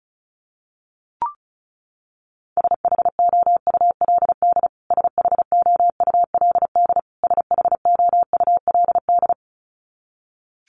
Morse code, heard as a keyed tone, about 34 wpm, 700 Hz. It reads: SHOULD SHOULD SHOULD
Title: Top 300 Words Repeated 35wpm